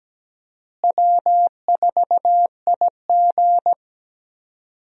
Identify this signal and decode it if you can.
Morse code, heard as a keyed tone, about 17 wpm, 700 Hz. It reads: W4IG
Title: USA Call Signs - 4 Characters - Repeated 3x - 17wpm